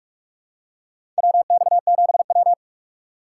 Morse code, heard as a keyed tone, about 47 wpm, 690 Hz. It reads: WX7W
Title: USA Call Signs - 4 Characters - 4x Character Spacing - 45wpm